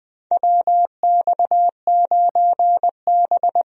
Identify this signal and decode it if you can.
Morse code, heard as a keyed tone, about 21 wpm, 700 Hz. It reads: WX9B
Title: USA Call Signs - 4 Characters - 20wpm